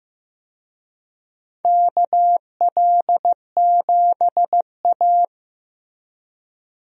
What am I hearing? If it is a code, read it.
Morse code, heard as a keyed tone, about 15 wpm, 700 Hz. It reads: KL7A
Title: DXCC-100 Call Signs - 4 Characters - Repeated 3x - 15wpm